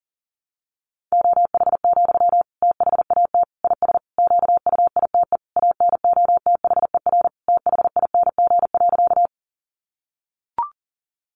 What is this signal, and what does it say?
Morse code, heard as a keyed tone, about 40 wpm, 700 Hz. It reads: OH, THAT IS QUITE ANOTHER THING.
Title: Sentences from Top 700 Words 40wpm